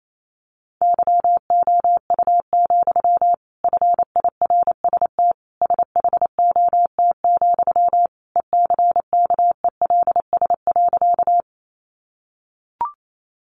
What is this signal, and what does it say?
Morse code, heard as a keyed tone, about 28 wpm, 700 Hz. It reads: YOU, FIRST SHOT, ECKELS.
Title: A Sound Of Thunder 28wpm